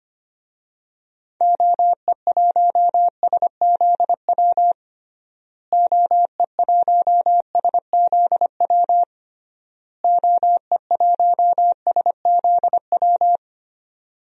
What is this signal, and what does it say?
Morse code, heard as a keyed tone, about 25 wpm, 700 Hz. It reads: OE1SZW OE1SZW OE1SZW
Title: DXCC-100 Call Signs - 6 Characters - Repeated 3x - 25wpm